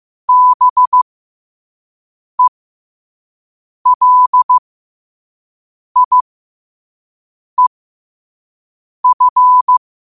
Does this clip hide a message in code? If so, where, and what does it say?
Yes: Morse code from 0.3 to 9.8 s, heard as a keyed tone, about 15 wpm, spaced as for 5 wpm, 1000 Hz. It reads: BELIEF